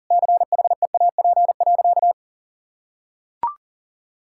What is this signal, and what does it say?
Morse code, heard as a keyed tone, about 40 wpm, 700 Hz. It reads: CHEAP.
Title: Sentences from Top 1600 Words 40wpm